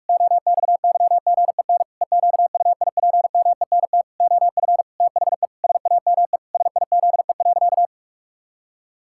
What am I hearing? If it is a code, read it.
Morse code, heard as a keyed tone, about 46 wpm, 700 Hz. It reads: OXYGEN EQUIPMENT OF THE SAME SIZE.
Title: Pangram Challenge v2 - 45wpm